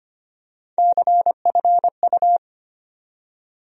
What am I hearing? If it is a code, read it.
Morse code, heard as a keyed tone, about 25 wpm, 700 Hz. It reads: CFU
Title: USA Call Sign Postfixes - 3 Letters 25wpm